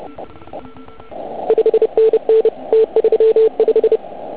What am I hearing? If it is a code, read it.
Morse code, heard as a keyed tone, about 27 wpm, 440 Hz. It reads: 5NN T35
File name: Archiv / CUC149 / OK1IF CUC149